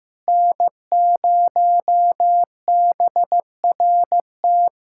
Morse code, heard as a keyed tone, about 15 wpm, 700 Hz. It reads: N0BRT
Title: USA Callsign DE Callsign - 5 Characters - 15wpm